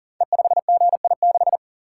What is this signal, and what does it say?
Morse code, heard as a keyed tone, about 42 wpm, 700 Hz. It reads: E5GI6
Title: Code Groups - 5 Letter:Numbers - 40wpm